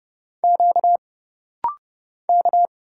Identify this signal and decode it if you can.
Morse code, heard as a keyed tone, about 30 wpm, 700 Hz. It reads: Q K
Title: Single Letter-Number Alt - Rapid-Fire 30wpm